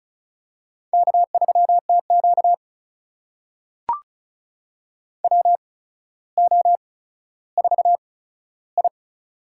Morse code, heard as a keyed tone, about 36 wpm, spaced as for 15 wpm, 700 Hz. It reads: K3TQ WO4I
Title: USA Call Signs - 4 Characters - 8x Character Spacing - 35wpm